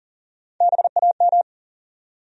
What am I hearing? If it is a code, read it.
Morse code, heard as a keyed tone, about 40 wpm, 700 Hz. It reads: BAM
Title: USA Call Sign Postfixes - 3 Letters 40wpm